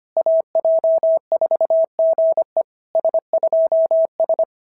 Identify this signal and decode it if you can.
Morse code, heard as a keyed tone, about 25 wpm, 650 Hz. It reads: AJ4GE S2S
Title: SOTA QSO Practice 25wpm